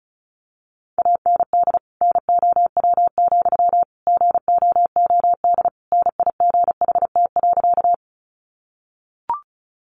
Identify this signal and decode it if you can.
Morse code, heard as a keyed tone, about 34 wpm, 700 Hz. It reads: AND NOW, GOOD NIGHT.
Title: Sentences from Top 300 Words 35wpm